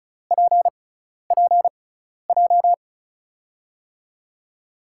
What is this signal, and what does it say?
Morse code, heard as a keyed tone, about 36 wpm, spaced as for 12 wpm, 700 Hz. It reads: PPJ
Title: USA Call Signs - 5 Characters - 6x Character Spacing - 35wpm